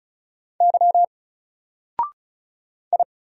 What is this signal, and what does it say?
Morse code, heard as a keyed tone, about 33 wpm, 700 Hz. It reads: Y I
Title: Single Letter-Number 35wpm